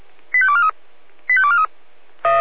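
Five-tone selective call (ZVEI), heard as quick runs of five tones, tones of about 70 ms. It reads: 85243 85243
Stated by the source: Five-tone ZVEI selective call received by SDR on 496.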